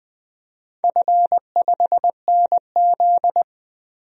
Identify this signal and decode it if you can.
Morse code, heard as a keyed tone, about 20 wpm, 700 Hz. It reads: F5NZ
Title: DXCC-100 Call Signs - 4 Characters - 20wpm